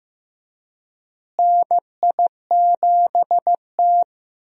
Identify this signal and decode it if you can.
Morse code, heard as a keyed tone, about 15 wpm, 700 Hz. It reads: NI7T
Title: USA Call Signs - 4 Characters - Repeated 3x - 15wpm